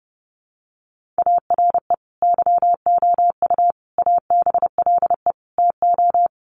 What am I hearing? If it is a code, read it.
Morse code, heard as a keyed tone, about 30 wpm, 700 Hz. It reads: ARE YOU ABLE TO
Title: Sentences from Top 1500 Words 30wpm